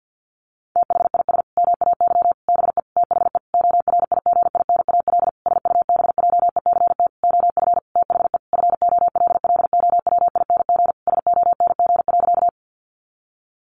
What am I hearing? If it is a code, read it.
Morse code, heard as a keyed tone, about 50 wpm, 700 Hz. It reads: THIS MAY BE THE ORIGINAL SUBJECT OF THE FOLLOWING SONG.
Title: Sentences from Top 900 Words 50wpm